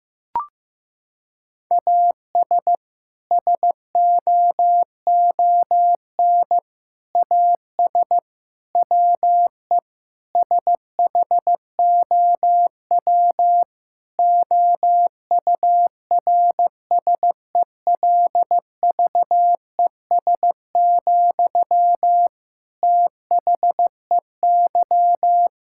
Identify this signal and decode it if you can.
Morse code, heard as a keyed tone, about 15 wpm, 700 Hz. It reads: AS SOON AS WE SHOW OURSELVES, THEY
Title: Sentences from Top 1600 Words 15wpm